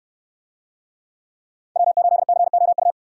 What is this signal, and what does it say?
Morse code, heard as a keyed tone, about 64 wpm, 700 Hz. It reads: W8COU